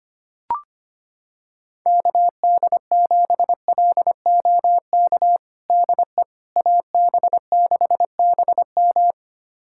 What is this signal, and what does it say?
Morse code, heard as a keyed tone, about 25 wpm, 700 Hz. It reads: KD7LOK DE AB6BM
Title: USA Callsign DE Callsign 25wpm